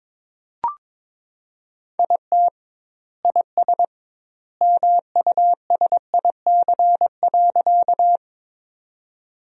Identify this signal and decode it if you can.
Morse code, heard as a keyed tone, about 22 wpm, 700 Hz. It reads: IT IS MUSIC.
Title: 2x Word-Spacing - Sentences from Top 300 Words - 22wpm